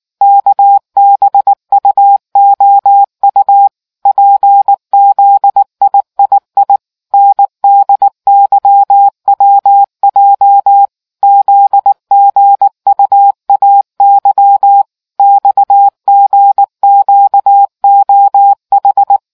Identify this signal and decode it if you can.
Morse code, heard as a keyed tone, about 19 wpm, 800 Hz. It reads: KBUOU PZIII NDYWJ ZGUAY XGQOH